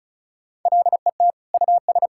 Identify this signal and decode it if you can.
Morse code, heard as a keyed tone, about 34 wpm, 700 Hz. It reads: LET US
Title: Sentences from Top 1600 Words 35wpm